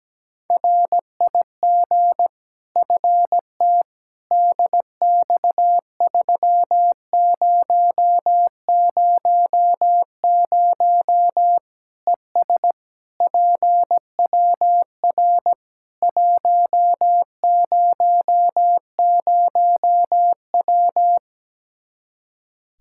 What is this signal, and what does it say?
Morse code, heard as a keyed tone, about 17 wpm, 700 Hz. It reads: RIG FT DX3000 ES PWR 100W
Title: Practice QSO - Exchange 2 - Info - 17wpm